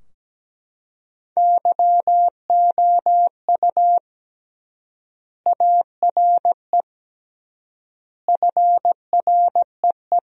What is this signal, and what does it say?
Morse code, heard as a keyed tone, about 17 wpm, 700 Hz. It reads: YOU ARE FREE
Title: 3x Word-Spacing - Sentences from Top 500 Words - 17wpm